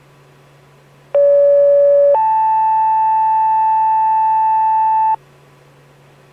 Two-tone page, heard as a one-second tone, then a three-second tone, about 570 then 880 Hz. It is heard from 1.1 to 5.1 s.